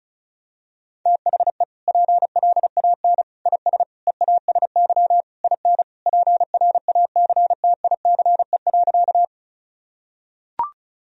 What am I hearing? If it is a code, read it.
Morse code, heard as a keyed tone, about 35 wpm, 700 Hz. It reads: THE PLAN IS EASY IN PRACTICE.
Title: Sentences from Top 1000 Words - Review Part 4 - 35wpm